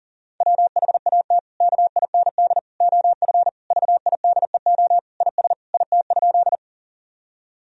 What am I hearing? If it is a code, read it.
Morse code, heard as a keyed tone, about 40 wpm, 700 Hz. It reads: WHAT KIND OF VIDEO IS IT?
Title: Sentences from Top 1000 Words - Review Part 4 - 40wpm